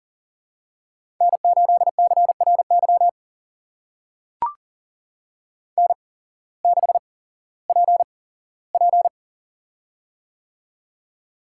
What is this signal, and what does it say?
Morse code, heard as a keyed tone, about 42 wpm, spaced as for 19 wpm, 700 Hz. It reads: N8CRY N6PP
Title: USA Call Signs - 8x Character Spacing - 40wpm